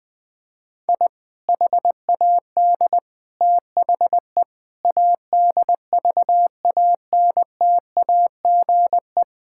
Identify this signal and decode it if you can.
Morse code, heard as a keyed tone, about 20 wpm, 700 Hz. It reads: I HAD THE ADVANTAGE
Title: Sentences from Top 1000 Words - Review Part 6 - 20wpm